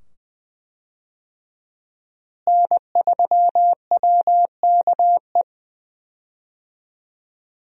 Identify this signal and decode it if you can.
Morse code, heard as a keyed tone, about 20 wpm, 700 Hz. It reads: N3WKE
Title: USA Call Signs - 5 Characters - Repeated 2x - 20wpm